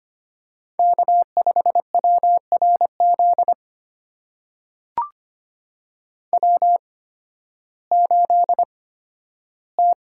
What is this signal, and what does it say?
Morse code, heard as a keyed tone, about 25 wpm, spaced as for 13 wpm, 700 Hz. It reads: K5WRZ W8T
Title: USA Call Signs - 5 Characters - 8x Character Spacing - 25wpm